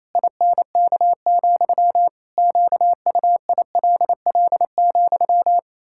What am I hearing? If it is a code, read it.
Morse code, heard as a keyed tone, about 27 wpm, 700 Hz. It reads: INK, QUILL,
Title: Pangram Challenge v2 - 28wpm